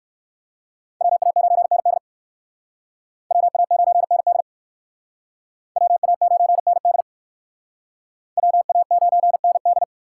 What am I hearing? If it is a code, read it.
Morse code, heard as a keyed tone, about 56 wpm, 700 Hz. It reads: WA9ND WA9ND WA9ND WA9ND